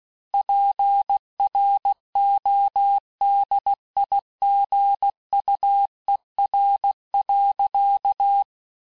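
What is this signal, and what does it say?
Morse code, heard as a keyed tone, about 16 wpm, 800 Hz. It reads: PRODIGUER.
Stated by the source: Morse course text, end of QSO, sent at 15 words per minute.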